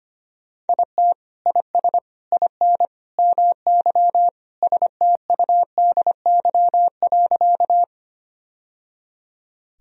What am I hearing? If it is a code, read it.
Morse code, heard as a keyed tone, about 25 wpm, 700 Hz. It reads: IT IS IN MY STUDY.
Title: Sentences from Top 1000 Words - Review Part 1 - 25wpm